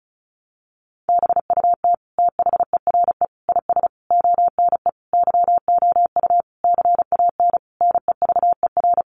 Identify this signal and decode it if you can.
Morse code, heard as a keyed tone, about 34 wpm, 700 Hz. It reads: BUT THERE IS ONE YOU CAN NEVER
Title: Sentences from Top 1200 Words 35wpm